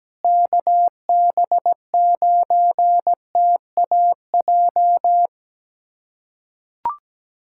Morse code, heard as a keyed tone, about 17 wpm, 700 Hz. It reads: KB9TAJ
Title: USA Callsign DE Callsign - 6 Characters - 17wpm